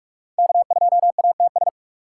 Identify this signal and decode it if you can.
Morse code, heard as a keyed tone, about 44 wpm, 700 Hz. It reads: K2ATS
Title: USA Call Signs - 5 Characters - 4x Character Spacing - 45wpm